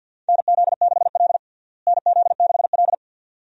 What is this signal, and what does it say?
Morse code, heard as a keyed tone, about 51 wpm, 700 Hz. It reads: NZ6L NZ6L
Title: USA Call Signs - 4 Characters - Repeated 2x - 50wpm